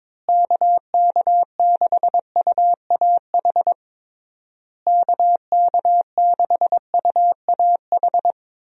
Morse code, heard as a keyed tone, about 22 wpm, 700 Hz. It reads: KK6UAH KK6UAH
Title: USA Call Signs - 6 Characters - Repeated 3x - 22wpm